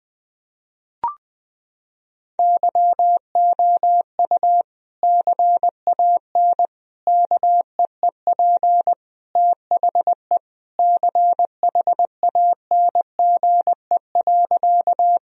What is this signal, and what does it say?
Morse code, heard as a keyed tone, about 20 wpm, 700 Hz. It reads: YOU CAN KEEP THE CHANGE.
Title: Sentences from Top 1000 Words - Review Part 1 - 20wpm